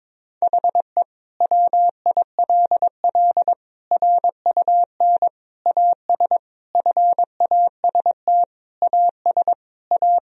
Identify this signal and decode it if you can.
Morse code, heard as a keyed tone, about 22 wpm, 700 Hz. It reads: HE WILL RUN AS FAST AS A
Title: Sentences from Top 1000 Words - Review Part 2 - 22wpm